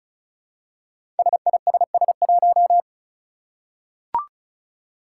Morse code, heard as a keyed tone, about 36 wpm, 700 Hz. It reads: SISS1
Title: Five Challenging Letter-Number Combinations 35wpm